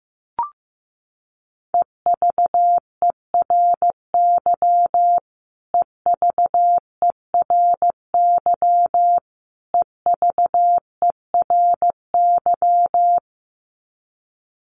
Morse code, heard as a keyed tone, about 15 wpm, 700 Hz. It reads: EVERY EVERY EVERY
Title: Top 500 Words Repeated 15wpm